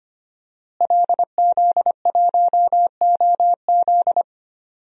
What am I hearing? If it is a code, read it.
Morse code, heard as a keyed tone, about 25 wpm, 700 Hz. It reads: LZ1OZ
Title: DXCC-100 Call Signs - 5 Characters - Repeated 2x - 25wpm